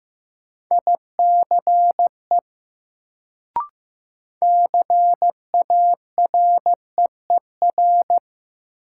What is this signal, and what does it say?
Morse code, heard as a keyed tone, about 15 wpm, 700 Hz. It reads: ICE CAREER
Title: Sentences from Top 1000 Words - New Word Drill - 15wpm